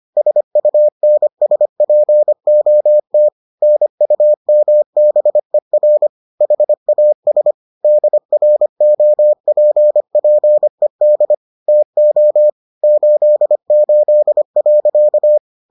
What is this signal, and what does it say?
Morse code, heard as a keyed tone, about 25 wpm, 600 Hz. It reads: SUNSPOT NUMBER HAS DROPPED TO 88.